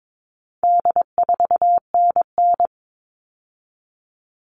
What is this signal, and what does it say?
Morse code, heard as a keyed tone, about 22 wpm, 700 Hz. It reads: D4NN
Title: All Cut Numbers - 4-Digit 22wpm